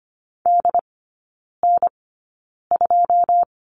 Morse code, heard as a keyed tone, about 25 wpm, spaced as for 9 wpm, 700 Hz. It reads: DN2
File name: Lesson 17e - Letter u - Three Characters 25wpm